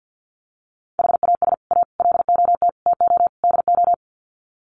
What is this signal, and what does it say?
Morse code, heard as a keyed tone, about 50 wpm, 700 Hz. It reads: HAS A LOT TO DO
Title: Sets of 5 Words Encore2 50wpm